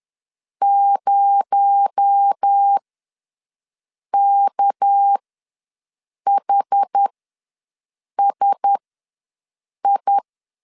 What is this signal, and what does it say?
Morse code, heard as a keyed tone, about 10 wpm, spaced as for 5 wpm, 800 Hz. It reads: 0KHSI
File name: LEZIONI / 6 PKX0 / 6 PKX0 ws 10wpm 4220